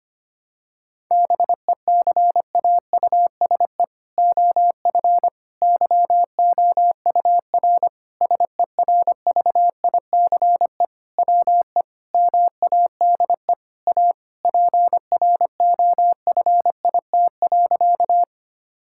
Morse code, heard as a keyed tone, about 25 wpm, 700 Hz. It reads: BECAUSE OF YOUR SERVICE WE MADE A PROFIT.
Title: Sentences from Top 1000 Words - Review Part 5 - 25wpm